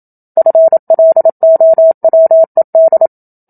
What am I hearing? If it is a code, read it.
Morse code, heard as a keyed tone, about 27 wpm, 650 Hz. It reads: FLOWED
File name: Book Two / 01-Chapter 1 - 27 WPM